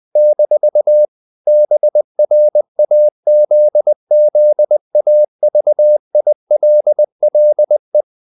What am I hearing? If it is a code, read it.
Morse code, heard as a keyed tone, about 20 wpm, 600 Hz. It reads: - BRAZZAVILLE
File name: CW / Practice / NationalCapitals20